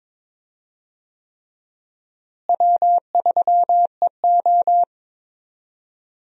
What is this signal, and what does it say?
Morse code, heard as a keyed tone, about 22 wpm, 700 Hz. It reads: W3EO
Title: USA Call Signs - 4 Characters - Repeated 3x - 22wpm